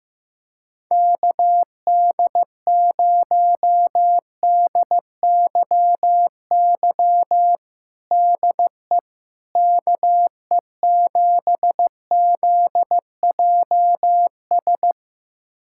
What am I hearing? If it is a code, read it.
Morse code, heard as a keyed tone, about 15 wpm, 700 Hz. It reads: KD0DYY DE KE7ZJS
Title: USA Callsign DE Callsign - 6 Characters - 15wpm